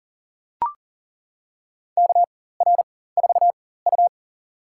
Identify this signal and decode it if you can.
Morse code, heard as a keyed tone, about 40 wpm, spaced as for 19 wpm, 700 Hz. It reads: KR4U
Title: USA Call Signs - 4 Characters - 4x Character Spacing - 40wpm